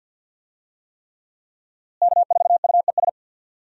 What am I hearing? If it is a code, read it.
Morse code, heard as a keyed tone, about 47 wpm, 700 Hz. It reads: K4UES
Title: USA Call Signs - 5 Characters - Repeated 3x - 50wpm